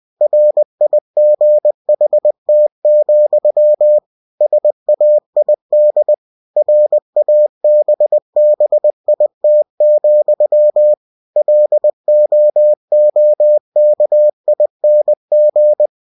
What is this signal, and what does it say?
Morse code, heard as a keyed tone, about 20 wpm, 600 Hz. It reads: RIGHT, SAID RABBIT, LOOKING
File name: cw-books / winnie-the-pooh / winnie-20wpm-page-0015